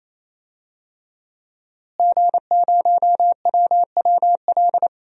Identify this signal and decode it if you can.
Morse code, heard as a keyed tone, about 28 wpm, 700 Hz. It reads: G0WWL
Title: DXCC-100 Call Signs - 5 Characters - Repeated 3x - 28wpm